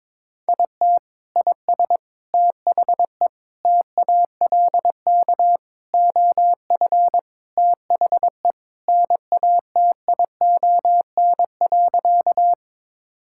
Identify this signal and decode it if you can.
Morse code, heard as a keyed tone, about 22 wpm, 700 Hz. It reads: IT IS THE TALK OF THE NATION.
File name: practice / 22wpm / Sentences with 4-Word Phrases - 22wpm